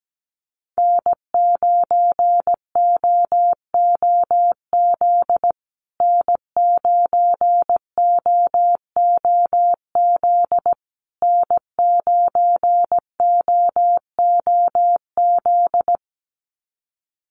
Morse code, heard as a keyed tone, about 17 wpm, 700 Hz. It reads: N9OOZ N9OOZ N9OOZ
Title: USA Call Signs Repeated 3x 17wpm